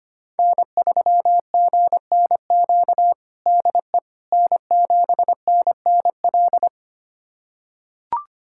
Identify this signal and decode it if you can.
Morse code, heard as a keyed tone, about 25 wpm, 700 Hz. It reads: N3GNQ DE N7NNL
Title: USA Callsign DE Callsign - 5 Characters - 25wpm